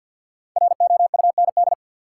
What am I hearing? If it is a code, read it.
Morse code, heard as a keyed tone, about 53 wpm, 700 Hz. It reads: ROUND
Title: Sets of 5 Words Encore - New Word Drill - 50wpm